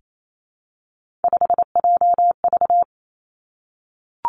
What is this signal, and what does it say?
Morse code, heard as a keyed tone, about 28 wpm, 700 Hz. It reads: 5JV
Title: Three Challenging Letter-Number Combinations 28wpm